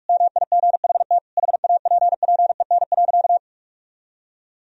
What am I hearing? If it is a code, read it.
Morse code, heard as a keyed tone, about 41 wpm, 700 Hz. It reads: MIGHT HAPPEN.
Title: Sentences from Top 400 Words and Abbreviations 45wpm